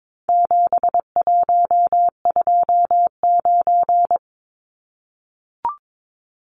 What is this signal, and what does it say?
Morse code, heard as a keyed tone, about 22 wpm, 700 Hz. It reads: 7129
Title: Numbers - 4-Digit 22wpm